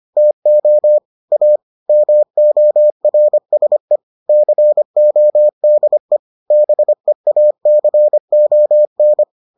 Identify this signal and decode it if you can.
Morse code, heard as a keyed tone, about 25 wpm, 600 Hz. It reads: TO A MORSE CODE BEACON